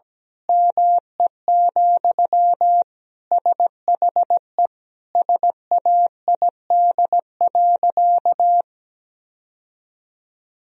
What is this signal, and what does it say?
Morse code, heard as a keyed tone, about 17 wpm, 700 Hz. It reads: ME, SHE SAID.
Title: Sentences from Top 1000 Words - Review Part 3 - 17wpm